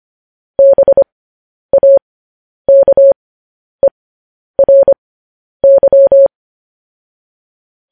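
Morse code, heard as a keyed tone, about 25 wpm, spaced as for 10 wpm, 550 Hz. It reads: BAKERY
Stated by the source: Morse code transmission of news headlines.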